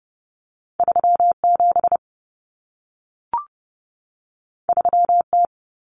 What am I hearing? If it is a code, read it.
Morse code, heard as a keyed tone, about 30 wpm, 700 Hz. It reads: 37 3T
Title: Common Cut Numbers - 2-Digit 30wpm